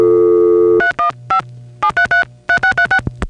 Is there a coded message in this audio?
Yes: DTMF from 0.8 to 3.0 s, heard as telephone touch tones, tones of about 100 ms. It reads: 315 7333636